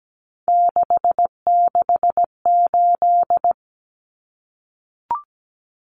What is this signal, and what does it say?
Morse code, heard as a keyed tone, about 17 wpm, 700 Hz. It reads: 668
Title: Numbers - 3-Digit 17wpm